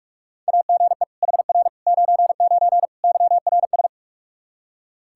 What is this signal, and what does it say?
Morse code, heard as a keyed tone, about 41 wpm, 700 Hz. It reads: AGE HR 99 YRS
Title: Practice QSO - Exchange 3 - More Info - 45wpm